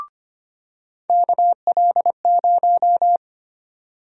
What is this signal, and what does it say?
Morse code, heard as a keyed tone, about 25 wpm, 700 Hz. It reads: KL0
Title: USA Call Sign Prefixes - 25wpm